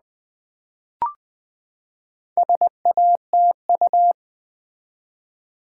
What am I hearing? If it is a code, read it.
Morse code, heard as a keyed tone, about 20 wpm, 700 Hz. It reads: SATU
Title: Build A Word - 4 Letters At A Time - 8-Letter Words - 20wpm